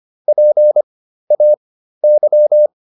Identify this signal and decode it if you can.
Morse code, heard as a keyed tone, about 25 wpm, spaced as for 13 wpm, 600 Hz. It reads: PAY